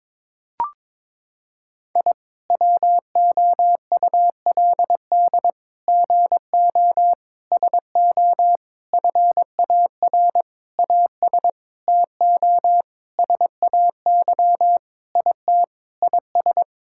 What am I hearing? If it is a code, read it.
Morse code, heard as a keyed tone, about 22 wpm, 700 Hz. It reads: I WOULD GO SO FAR AS TO SAY IT IS
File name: practice / 22wpm / Sentences with 8-Word Phrases - 22wpm